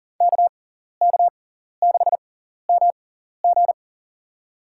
Morse code, heard as a keyed tone, about 40 wpm, spaced as for 14 wpm, 700 Hz. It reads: KK6MG
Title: USA Call Signs - 5 Characters - 6x Character Spacing - 40wpm